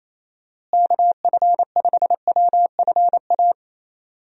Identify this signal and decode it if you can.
Morse code, heard as a keyed tone, about 28 wpm, 700 Hz. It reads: KF5WFA
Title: USA Call Signs - 6 Characters - Repeated 2x - 28wpm